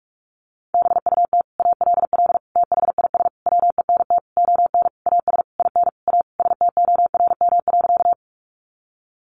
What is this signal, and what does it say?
Morse code, heard as a keyed tone, about 44 wpm, 700 Hz. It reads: BUT ALL THIS WENT ON AS IN A STORM.
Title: Sentences from Top 1500 Words 45wpm